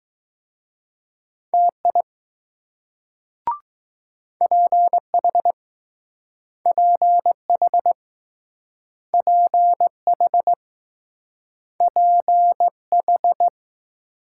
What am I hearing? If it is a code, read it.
Morse code, heard as a keyed tone, about 20 wpm, 700 Hz. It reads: TI PH PH PH PH